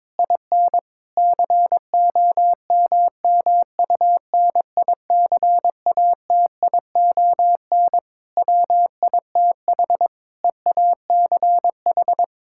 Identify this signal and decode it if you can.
Morse code, heard as a keyed tone, about 22 wpm, 700 Hz. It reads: IN COMMUNICATION WITH EACH
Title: Sentences from Top 1000 Words - Review Part 6 - 22wpm